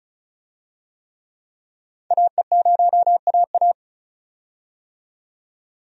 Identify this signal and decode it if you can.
Morse code, heard as a keyed tone, about 36 wpm, 700 Hz. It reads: AE0AA
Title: USA Call Signs - 5 Characters - 35wpm